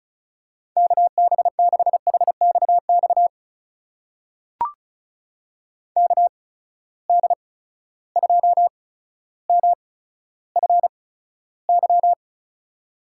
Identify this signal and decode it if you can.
Morse code, heard as a keyed tone, about 36 wpm, spaced as for 15 wpm, 700 Hz. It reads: KB6HXX KD2MFY
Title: USA Call Signs - 6 Characters - 8x Character Spacing - 35wpm